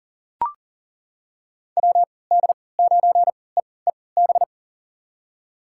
Morse code, heard as a keyed tone, about 40 wpm, spaced as for 23 wpm, 700 Hz. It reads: WD9EEB
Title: USA Call Signs - 6 Characters - 3x Character Spacing - 40wpm